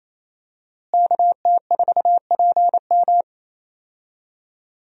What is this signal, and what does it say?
Morse code, heard as a keyed tone, about 28 wpm, 700 Hz. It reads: KT4PM
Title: USA Call Signs - 5 Characters - Repeated 2x - 28wpm